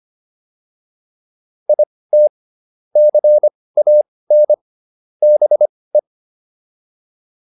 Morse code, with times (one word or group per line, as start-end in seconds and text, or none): 1.7-2.3 IT
3.0-4.5 CAN
5.2-6.0 BE